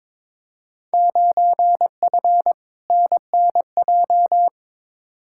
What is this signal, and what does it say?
Morse code, heard as a keyed tone, about 22 wpm, 700 Hz. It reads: 9F NNJ
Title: Field Day Exchanges Encore - 22wpm